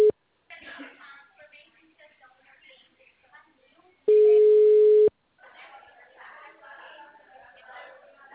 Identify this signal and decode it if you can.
Ringback tone (Europe, 425 Hz), with speech.
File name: out- -2005- - - .43001